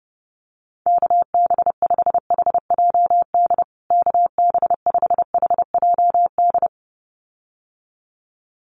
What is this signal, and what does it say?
Morse code, heard as a keyed tone, about 30 wpm, 700 Hz. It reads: KB5HJD KB5HJD
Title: USA Call Signs Repeated 2x 30wpm